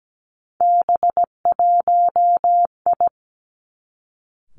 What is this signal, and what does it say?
Morse code, heard as a keyed tone, about 17 wpm, 700 Hz. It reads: B1I
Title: Three Challenging Letter-Number Combinations 17wpm